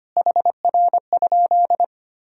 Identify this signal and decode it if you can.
Morse code, heard as a keyed tone, about 25 wpm, 700 Hz. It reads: HR?
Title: Sentences from Top 200 Words and Abbreviations 25wpm